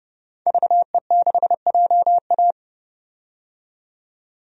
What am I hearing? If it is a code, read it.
Morse code, heard as a keyed tone, about 30 wpm, 700 Hz. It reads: VE6JA
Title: DXCC-100 Call Signs - 5 Characters - 30wpm